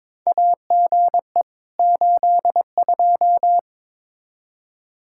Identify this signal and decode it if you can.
Morse code, heard as a keyed tone, about 22 wpm, 700 Hz. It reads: AGE 82
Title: K1USN SST QSO Practice 22wpm